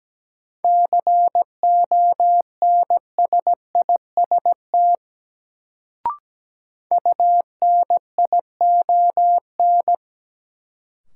Morse code, heard as a keyed tone, about 17 wpm, 700 Hz. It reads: CONSIST UNION
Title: Sentences from Top 1400 Words - New Word Drill Alt - 17wpm